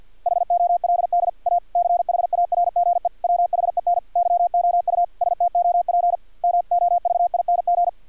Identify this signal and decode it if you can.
Morse code, heard as a keyed tone, about 50 wpm, 700 Hz. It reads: HOLD A CHARGE WHEN YOU STOP MOVING
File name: archived / CRSnet